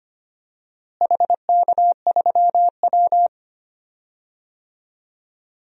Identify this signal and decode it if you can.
Morse code, heard as a keyed tone, about 25 wpm, 700 Hz. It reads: HK3W
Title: DXCC-100 Call Signs - 4 Characters - Repeated 3x - 25wpm